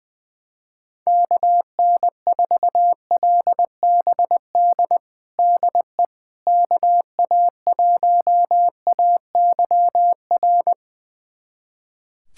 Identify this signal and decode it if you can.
Morse code, heard as a keyed tone, about 20 wpm, 700 Hz. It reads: KN4LBD DE KA1AYR
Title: USA Callsign DE Callsign - 6 Characters - 20wpm